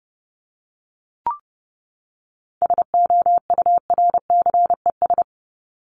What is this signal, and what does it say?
Morse code, heard as a keyed tone, about 30 wpm, 700 Hz. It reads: SOURCES
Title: ES:S Suffix - Top 100 Words with Base Words 30wpm